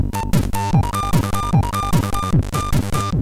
There is no code.